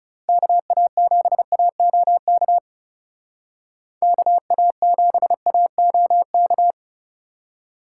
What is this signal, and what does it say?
Morse code, heard as a keyed tone, about 37 wpm, 700 Hz. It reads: KA7AOK KA7AOK
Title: Speed Racing - USA Call Signs 30wpm